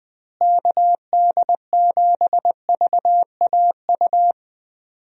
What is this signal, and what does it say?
Morse code, heard as a keyed tone, about 21 wpm, 700 Hz. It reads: KD7VAU
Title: USA Call Signs - 6 Characters - Repeated 2x - 20wpm